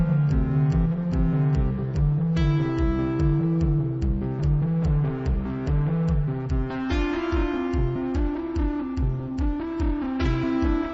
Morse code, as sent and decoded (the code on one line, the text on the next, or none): none